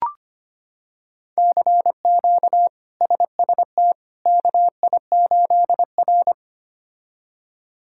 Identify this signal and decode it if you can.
Morse code, heard as a keyed tone, about 25 wpm, 700 Hz. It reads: CQ SST KI8R
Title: K1USN SST QSO Practice - 25wpm